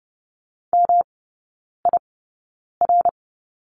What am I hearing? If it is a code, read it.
Morse code, heard as a keyed tone, about 30 wpm, spaced as for 9 wpm, 700 Hz. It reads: MIR
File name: Lesson 18e - Letter m - Three Characters 30wpm